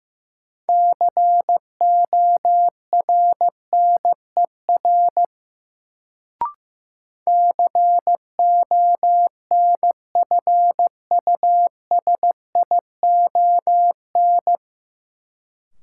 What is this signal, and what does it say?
Morse code, heard as a keyed tone, about 15 wpm, 700 Hz. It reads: CORNER CONFUSION
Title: Sentences from Top 1600 Words - New Word Drill - 15wpm